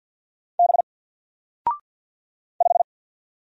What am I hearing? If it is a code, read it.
Morse code, heard as a keyed tone, about 49 wpm, 700 Hz. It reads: B 5
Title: Single Character (Letter:Number:Punctuation:Prosign) 50wpm